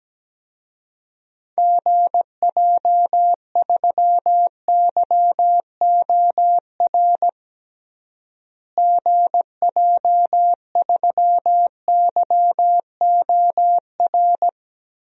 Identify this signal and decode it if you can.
Morse code, heard as a keyed tone, about 17 wpm, 700 Hz. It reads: GJ3YOR GJ3YOR
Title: DXCC-100 Call Signs - 6 Characters - Repeated 3x - 17wpm